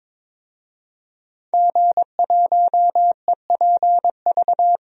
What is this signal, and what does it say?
Morse code, heard as a keyed tone, about 22 wpm, 700 Hz. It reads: G1EPV
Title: DXCC-100 Call Signs - 5 Characters - Repeated 2x - 22wpm